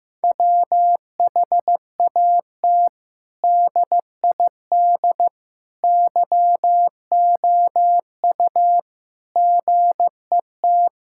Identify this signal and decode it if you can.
Morse code, heard as a keyed tone, about 15 wpm, 700 Hz. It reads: WHAT DID YOU GET
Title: Sentences from Top 1000 Words - Review Part 6 - 15wpm